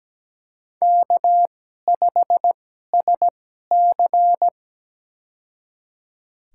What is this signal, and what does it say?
Morse code, heard as a keyed tone, about 17 wpm, spaced as for 12 wpm, 700 Hz. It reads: K5SC
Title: USA Call Signs - 4 Characters - 2x Character Spacing - 17wpm